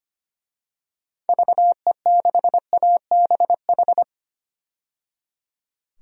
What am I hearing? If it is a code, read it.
Morse code, heard as a keyed tone, about 25 wpm, 700 Hz. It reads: VE6ABH
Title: DXCC-100 Call Signs - 6 Characters - 25wpm